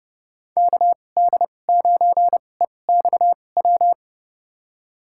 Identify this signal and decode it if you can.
Morse code, heard as a keyed tone, about 30 wpm, spaced as for 22 wpm, 700 Hz. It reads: KD9EXW
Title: USA Call Signs - 2x Character Spacing - 30wpm